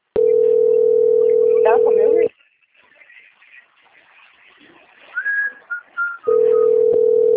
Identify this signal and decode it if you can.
Ringback tone (North America), 440 + 480 Hz, with speech.
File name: rg-1600-unknown- - - .36606